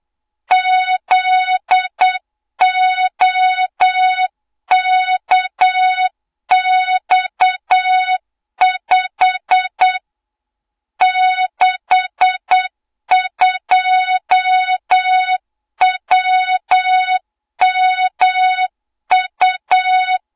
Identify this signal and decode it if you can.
Morse code, heard as a keyed tone, about 8 wpm, 3700 Hz, with noise. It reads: ZOKX5 62WMU
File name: Morze-HA2MN / hang / 20 GYAKORLAT